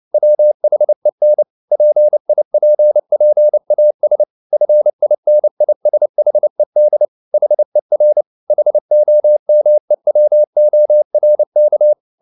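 Morse code, heard as a keyed tone, about 29 wpm, 600 Hz. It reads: WHEN PIPPAS FINISHED HER HOMEWORK
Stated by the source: Morse code at 29 wpm, sending Ken Follett's novel Never in English, from CW med Gnister.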